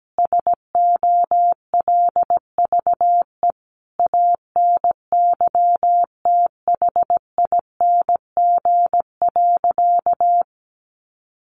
Morse code, heard as a keyed tone, about 17 wpm, 700 Hz. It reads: SOLVE ANYTHING.
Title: Sentences from Top 1000 Words 17wpm